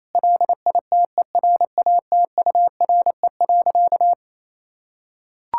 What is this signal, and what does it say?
Morse code, heard as a keyed tone, about 27 wpm, 700 Hz. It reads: LITERATURE.
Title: Sentences from Top 1000 Words - Review Part 6 - 28wpm